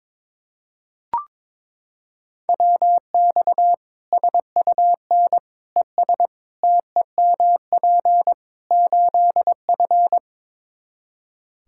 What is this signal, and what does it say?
Morse code, heard as a keyed tone, about 22 wpm, 700 Hz. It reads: WX SUN ES TEMP 8F
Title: Practice QSO - Exchange 2 - Info - 22wpm